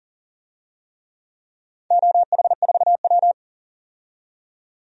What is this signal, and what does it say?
Morse code, heard as a keyed tone, about 40 wpm, 700 Hz. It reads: OH4W
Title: DXCC-100 Call Signs - 4 Characters - 40wpm